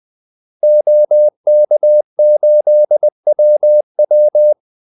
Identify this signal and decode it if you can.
Morse code, heard as a keyed tone, about 20 wpm, 600 Hz. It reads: OK8WW
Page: Kaldesignaler 20 wpm